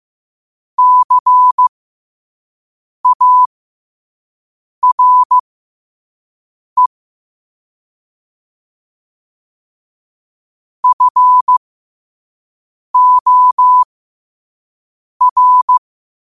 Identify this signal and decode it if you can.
Morse code, heard as a keyed tone, about 15 wpm, spaced as for 5 wpm, 1000 Hz. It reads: CARE FOR
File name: qotdfiles / 5wpm